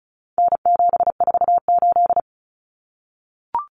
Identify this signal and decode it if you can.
Morse code, heard as a keyed tone, about 34 wpm, 700 Hz. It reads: N748
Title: Common Cut Numbers - 4-Digit 35wpm